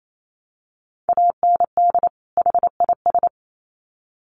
Morse code, heard as a keyed tone, about 28 wpm, 700 Hz. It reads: AND HIS
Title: Sets of 2 Words 28wpm